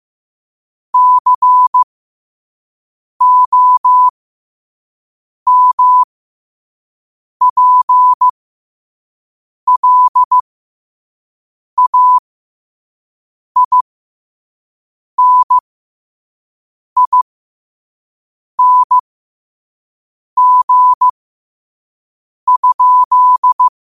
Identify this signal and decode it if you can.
Morse code, heard as a keyed tone, about 15 wpm, spaced as for 5 wpm, 1000 Hz. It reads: COMPLAINING?
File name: qotdfiles / 5wpm